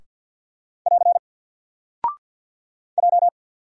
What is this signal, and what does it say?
Morse code, heard as a keyed tone, about 50 wpm, 700 Hz. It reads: <AR> J
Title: Single Character (Letter:Number:Punctuation:Prosign) 50wpm